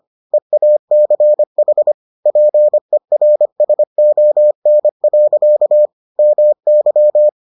Morse code, heard as a keyed tone, about 25 wpm, 600 Hz. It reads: EACH PERSON. MY